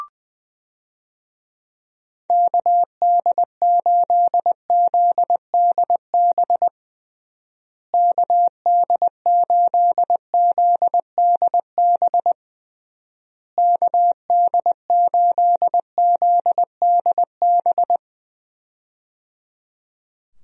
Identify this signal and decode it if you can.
Morse code, heard as a keyed tone, about 20 wpm, 700 Hz. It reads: KD8ZDB KD8ZDB KD8ZDB
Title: USA Call Signs - 6 Characters - Repeated 3x - 20wpm